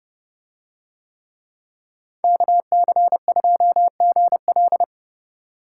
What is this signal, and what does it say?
Morse code, heard as a keyed tone, about 31 wpm, 700 Hz. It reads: KC2GL